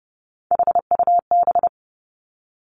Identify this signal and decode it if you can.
Morse code, heard as a keyed tone, about 30 wpm, 700 Hz. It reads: HUB
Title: Three Challenging Letter-Number Combinations 30wpm